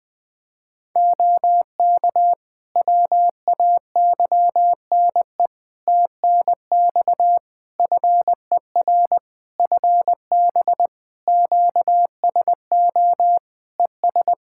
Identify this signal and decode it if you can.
Morse code, heard as a keyed tone, about 20 wpm, 700 Hz. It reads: OK WAYNE TNX FER FB QSO ES
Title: Practice QSO - Exchange 4 - Ending - 20wpm